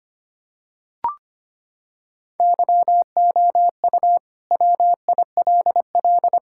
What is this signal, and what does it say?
Morse code, heard as a keyed tone, about 25 wpm, 700 Hz. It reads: YOU WILL
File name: practice / 25wpm / Sentences with 9-Word Phrases - 25wpm